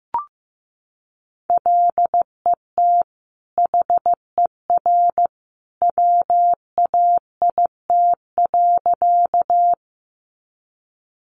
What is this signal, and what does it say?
Morse code, heard as a keyed tone, about 15 wpm, 700 Hz. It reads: LET HER WAIT.
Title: Sentences from Top 500 Words 15wpm